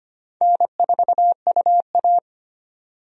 Morse code, heard as a keyed tone, about 25 wpm, 700 Hz. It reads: N4UA
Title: USA Call Signs - 4 Characters - 25wpm